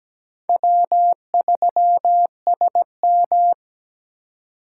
Morse code, heard as a keyed tone, about 20 wpm, 700 Hz. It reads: W3SM
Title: Speed Racing - USA Call Signs 17wpm